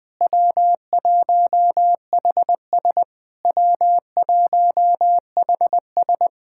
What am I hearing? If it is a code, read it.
Morse code, heard as a keyed tone, about 21 wpm, 690 Hz. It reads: W1HS W1HS
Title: SOTA QSO Practice 20wpm